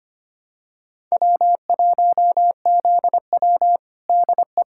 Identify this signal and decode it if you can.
Morse code, heard as a keyed tone, about 25 wpm, 700 Hz. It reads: W1ZW DE
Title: USA Callsign DE Callsign - 4 Characters - 25wpm